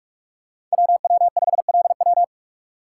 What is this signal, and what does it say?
Morse code, heard as a keyed tone, about 44 wpm, 700 Hz. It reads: WW5LW
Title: USA Call Signs - 5 Characters - 3x Character Spacing - 45wpm